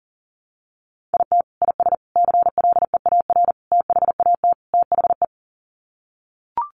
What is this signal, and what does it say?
Morse code, heard as a keyed tone, about 37 wpm, 700 Hz. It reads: IT IS CLEAR THAT THE
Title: Sets of 5 Words Encore 40wpm